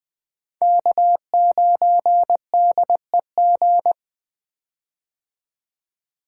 Morse code, heard as a keyed tone, about 21 wpm, 700 Hz. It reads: K9DEG
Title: 2023 POTA Hunter Call Signs - 20wpm